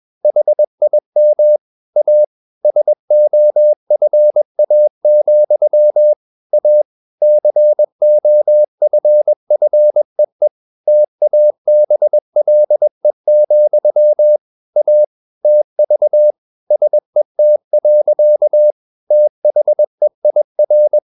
Morse code, heard as a keyed tone, about 21 wpm, 600 Hz. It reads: HIM A SOFA, A COFFEE TABLE, A TV SET. THEIR
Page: CW med Gnister